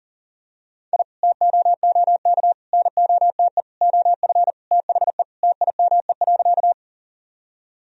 Morse code, heard as a keyed tone, about 37 wpm, 700 Hz. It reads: I TOOK NOTE OF THE TIME.
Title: Sentences from Top 1000 Words - Review Part 2 - 40wpm